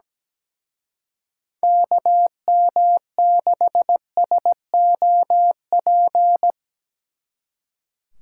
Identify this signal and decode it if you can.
Morse code, heard as a keyed tone, about 19 wpm, 700 Hz. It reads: KM6SOP
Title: Speed Racing - USA 2x3 Call Signs 17wpm